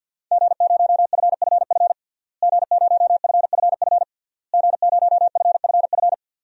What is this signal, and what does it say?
Morse code, heard as a keyed tone, about 51 wpm, 700 Hz. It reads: G0FFF G0FFF G0FFF
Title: DXCC-100 Call Signs - 5 Characters - Repeated 3x - 50wpm